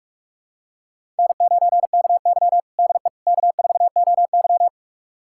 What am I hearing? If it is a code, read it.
Morse code, heard as a keyed tone, about 42 wpm, 700 Hz. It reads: N9KY DE K4OY
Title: USA Callsign DE Callsign - 4 Characters - 45wpm